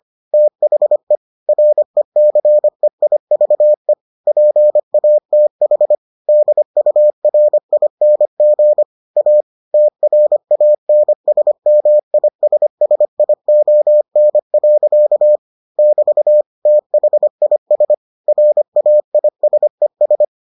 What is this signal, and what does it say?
Morse code, heard as a keyed tone, about 25 wpm, 600 Hz. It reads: THE RECEIVE PATH DURING A TRANSMISSION. <BT> THIS RAISES